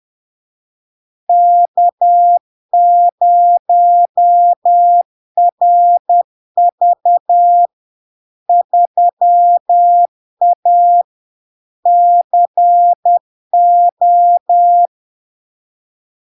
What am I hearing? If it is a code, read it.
Morse code, heard as a keyed tone, about 10 wpm, 700 Hz. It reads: K0RV 3A CO
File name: LARK-FD-2022-Log-in-CW-10wpm-700-Hz